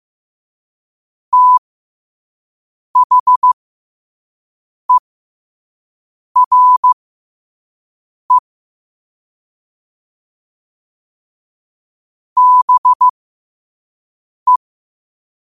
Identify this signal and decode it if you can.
Morse code, heard as a keyed tone, about 15 wpm, spaced as for 5 wpm, 1000 Hz. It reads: THERE BE
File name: qotdfiles / 5wpm